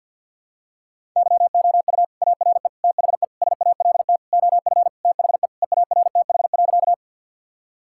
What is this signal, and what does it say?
Morse code, heard as a keyed tone, about 53 wpm, 700 Hz. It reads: YOU ARE THE SALT OF THE EARTH.
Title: Sentences from Top 1000 Words - Review Part 5 - 50wpm